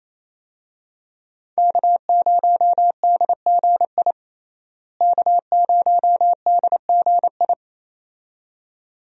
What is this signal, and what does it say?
Morse code, heard as a keyed tone, about 28 wpm, 700 Hz. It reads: K0DGI K0DGI
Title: USA Call Signs - 5 Characters - Repeated 2x - 28wpm